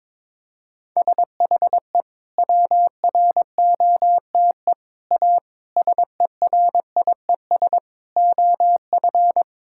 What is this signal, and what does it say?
Morse code, heard as a keyed tone, about 22 wpm, 700 Hz. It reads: SHE WROTE A SERIES OF
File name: practice / 22wpm / Sentences with 3-Word Phrases - 22wpm